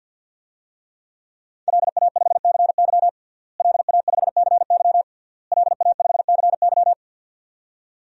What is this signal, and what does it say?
Morse code, heard as a keyed tone, about 51 wpm, 700 Hz. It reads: LA5CY LA5CY LA5CY
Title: DXCC-100 Call Signs - 5 Characters - Repeated 3x - 50wpm